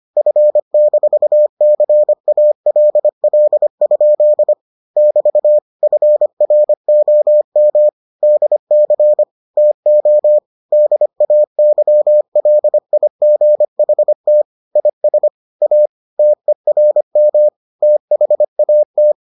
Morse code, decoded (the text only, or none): F-CALL? <BT> FROM DC TO DAYLIGHT IS A TERM THAT